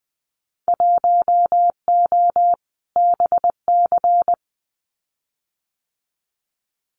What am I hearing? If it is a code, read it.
Morse code, heard as a keyed tone, about 20 wpm, 700 Hz. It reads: 1O BC
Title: Winter Field Day Exchanges Encore 20wpm